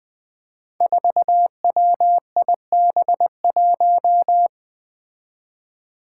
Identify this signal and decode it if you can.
Morse code, heard as a keyed tone, about 20 wpm, 700 Hz. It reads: 4WIB1
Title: Five Challenging Letter-Number Combinations 20wpm